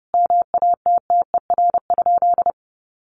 Morse code, heard as a keyed tone, about 30 wpm, 700 Hz. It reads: MATTER?
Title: Sentences from Top 700 Words 30wpm